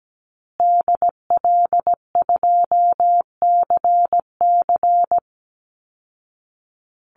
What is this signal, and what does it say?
Morse code, heard as a keyed tone, about 17 wpm, 700 Hz. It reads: DL2CC
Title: Call Signs Repeated 2x - 17wpm